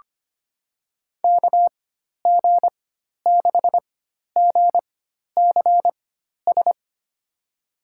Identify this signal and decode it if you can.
Morse code, heard as a keyed tone, about 25 wpm, spaced as for 12 wpm, 700 Hz. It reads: KG6GCS
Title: USA Call Signs - 4x Character Spacing - 25wpm